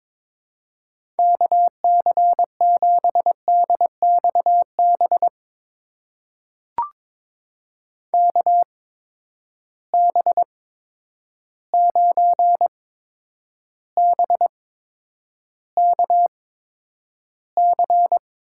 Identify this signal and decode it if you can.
Morse code, heard as a keyed tone, about 23 wpm, spaced as for 9 wpm, 700 Hz. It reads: KC7DXB KB9BKC
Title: USA Call Signs - 6 Characters - 8x Character Spacing - 22wpm